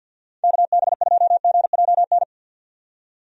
Morse code, heard as a keyed tone, about 51 wpm, 700 Hz. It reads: KB2GJN
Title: USA Call Signs - 6 Characters - 50wpm